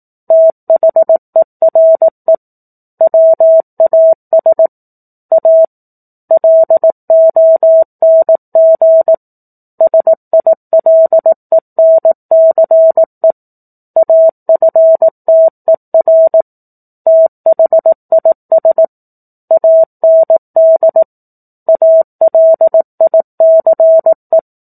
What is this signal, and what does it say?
Morse code, heard as a keyed tone, about 18 wpm, 650 Hz. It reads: THERE WAS A LONG SILENCE AFTER THIS AND ALICE
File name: 04-Chapter-04 18 WPM